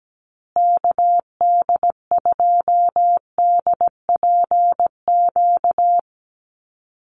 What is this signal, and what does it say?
Morse code, heard as a keyed tone, about 17 wpm, 700 Hz. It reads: KD2DPQ
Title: USA Call Signs Repeated 2x 17wpm